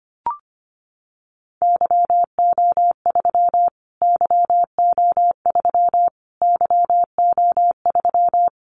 Morse code, heard as a keyed tone, about 25 wpm, 700 Hz. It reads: YO3 YO3 YO3
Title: Callsign Prefixes for Top 25 DXCC - Repeated 3x - 25wpm